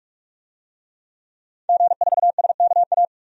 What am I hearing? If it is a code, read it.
Morse code, heard as a keyed tone, about 42 wpm, 700 Hz. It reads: G4SKA
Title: DXCC-100 Call Signs - 5 Characters - 45wpm